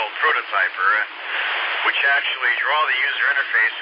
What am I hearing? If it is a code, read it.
Emergency Alert System (SAME) tones, heard as harsh buzzing bursts, with speech. It is heard from 0.5 to 3.5 s.